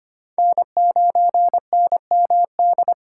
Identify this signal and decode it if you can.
Morse code, heard as a keyed tone, about 31 wpm, 700 Hz. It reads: N9NMD
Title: Speed Racing - USA 1x3 Call Signs 25wpm